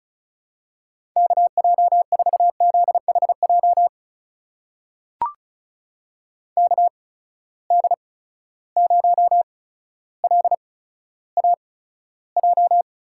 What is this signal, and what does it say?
Morse code, heard as a keyed tone, about 36 wpm, spaced as for 15 wpm, 700 Hz. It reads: KJ4ZHJ KD0LAJ
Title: USA Call Signs - 6 Characters - 8x Character Spacing - 35wpm